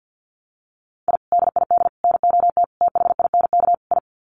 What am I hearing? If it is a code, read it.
Morse code, heard as a keyed tone, about 50 wpm, 700 Hz. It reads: I DID NOT THINK I
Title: Sets of 5 Words Encore 50wpm